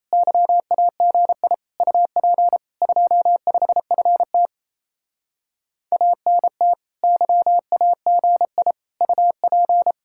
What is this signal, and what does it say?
Morse code, heard as a keyed tone, about 35 wpm, 700 Hz. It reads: YAGI UP 25FT ANT YAGI UP